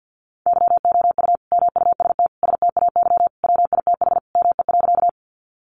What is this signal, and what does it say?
Morse code, heard as a keyed tone, about 50 wpm, 700 Hz. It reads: YOU MUST STAY WITH ME.
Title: Sentences from Top 500 Words 50wpm